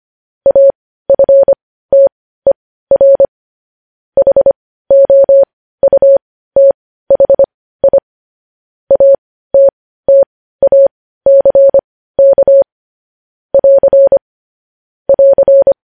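Morse code, heard as a keyed tone, about 25 wpm, spaced as for 15 wpm, 550 Hz. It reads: AFTER HOUTHI ATTACK <AR> <AR>
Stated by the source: Morse code news headlines practice at 15, preceded by vvv.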